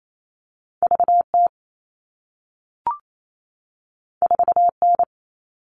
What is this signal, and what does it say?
Morse code, heard as a keyed tone, about 28 wpm, 700 Hz. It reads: VT 4N
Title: All Cut Numbers - 2-Digit 28wpm